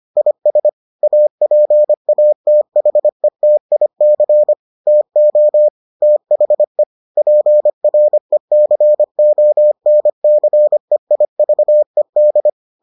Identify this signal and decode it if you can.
Morse code, heard as a keyed tone, about 25 wpm, 600 Hz. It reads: IS APATHETIC TO THE PRECONCEIVED